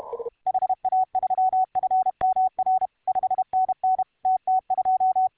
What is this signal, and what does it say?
Morse code, heard as a keyed tone, about 31 wpm, 750 Hz. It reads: HA3FMR 5NN TT2